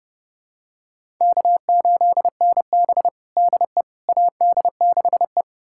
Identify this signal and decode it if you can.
Morse code, heard as a keyed tone, about 32 wpm, 700 Hz. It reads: K8NB DE AD6E
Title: USA Callsign DE Callsign - 4 Characters - 30wpm